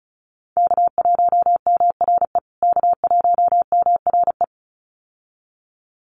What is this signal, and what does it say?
Morse code, heard as a keyed tone, about 34 wpm, 700 Hz. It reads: K1MRE K1MRE
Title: USA Call Signs Repeated 2x 35wpm